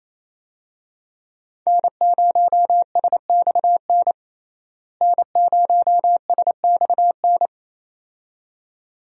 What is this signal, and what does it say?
Morse code, heard as a keyed tone, about 28 wpm, 700 Hz. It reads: N0SXN N0SXN
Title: USA Call Signs - 5 Characters - Repeated 2x - 28wpm